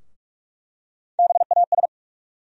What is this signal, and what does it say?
Morse code, heard as a keyed tone, about 43 wpm, 700 Hz. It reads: BAS
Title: USA Call Sign Postfixes - 3 Letters 45wpm